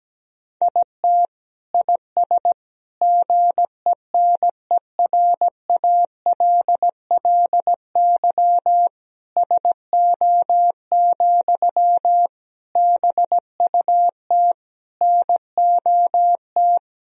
Morse code, heard as a keyed tone, about 17 wpm, 700 Hz. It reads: IT IS GENERALLY SO, BUT NOT
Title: Sentences from Top 1000 Words - Review Part 4 - 17wpm